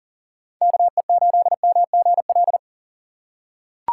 Morse code, heard as a keyed tone, about 40 wpm, 700 Hz. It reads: KE8MGL
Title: USA Call Signs - 6 Characters - 2x Character Spacing - 40wpm